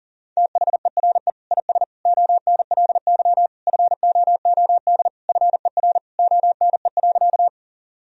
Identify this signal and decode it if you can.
Morse code, heard as a keyed tone, about 40 wpm, 700 Hz. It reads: THERE IS ONLY FOOD FER ONE.
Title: Sentences from Top 300 Words and Abbreviations 40wpm